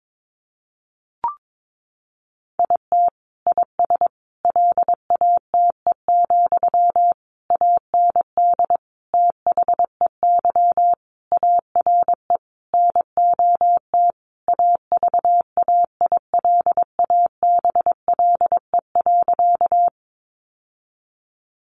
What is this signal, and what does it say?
Morse code, heard as a keyed tone, about 22 wpm, 700 Hz. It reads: IT IS LATE, AND THEY ARE NOT AVAILABLE.
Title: Sentences from Top 600 Words 22wpm